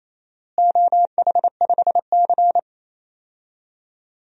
Morse code, heard as a keyed tone, about 28 wpm, 700 Hz. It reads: OH5C
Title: DXCC-100 Call Signs - 4 Characters - Repeated 2x - 28wpm